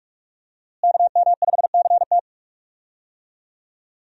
Morse code, heard as a keyed tone, about 43 wpm, 700 Hz. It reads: KM5CT
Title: USA Call Signs - 5 Characters - Repeated 2x - 45wpm